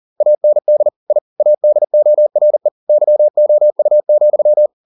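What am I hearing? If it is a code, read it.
Morse code, heard as a keyed tone, about 40 wpm, 600 Hz. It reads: AND I ADORE YOU,
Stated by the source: Morse code at 40 wpm, practice text, CW med Gnister.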